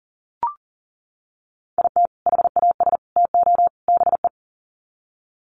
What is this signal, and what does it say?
Morse code, heard as a keyed tone, about 40 wpm, 700 Hz. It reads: IT HAS TO BE
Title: Sets of 4 Words 40wpm